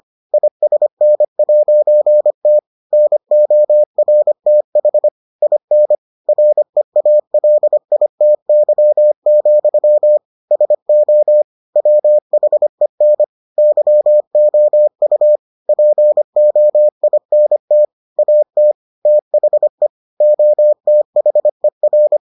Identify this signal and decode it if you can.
Morse code, heard as a keyed tone, about 25 wpm, 600 Hz. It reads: ISN'T NORTH IN REALITY, SO WHEN YOU POINT AT THE OTHER